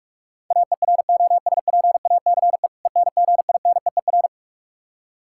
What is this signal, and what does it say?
Morse code, heard as a keyed tone, about 46 wpm, 700 Hz. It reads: AEROSPACE ENGINEER
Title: K1USN SST QSO Practice 45wpm